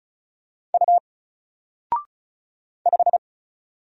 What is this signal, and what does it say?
Morse code, heard as a keyed tone, about 36 wpm, 700 Hz. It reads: U 5
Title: Single Letter-Number Alt 35wpm